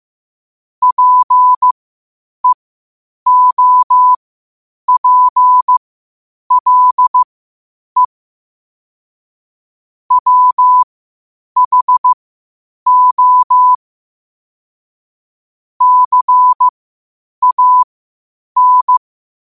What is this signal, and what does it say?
Morse code, heard as a keyed tone, about 15 wpm, spaced as for 8 wpm, 1000 Hz. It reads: PEOPLE WHO CAN
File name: 8wpm